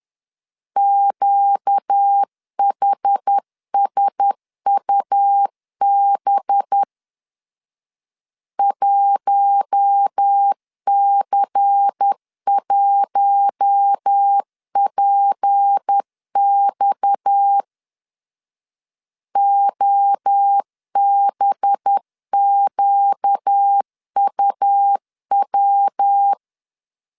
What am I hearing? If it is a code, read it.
Morse code, heard as a keyed tone, about 10 wpm, 800 Hz. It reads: QHSUB 1C1PX OBQUW
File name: CW / DET / 1 6 1500C S A 10wpm 4141